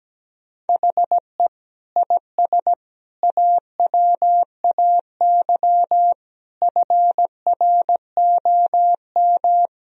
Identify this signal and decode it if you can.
Morse code, heard as a keyed tone, about 17 wpm, 700 Hz. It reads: HE IS AWAY FROM
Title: Sentences from Top 1000 Words - Review Part 1 - 17wpm